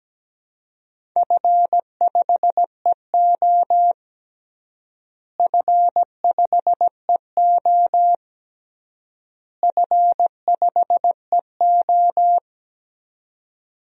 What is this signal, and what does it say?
Morse code, heard as a keyed tone, about 17 wpm, 700 Hz. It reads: F5EO F5EO F5EO
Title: DXCC-100 Call Signs - 4 Characters - Repeated 3x - 17wpm